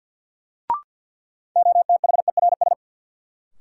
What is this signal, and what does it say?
Morse code, heard as a keyed tone, about 47 wpm, 700 Hz. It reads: OTHERS
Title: Sentences from Top 1000 Words - Review Part 2 - New Word Drill - 50wpm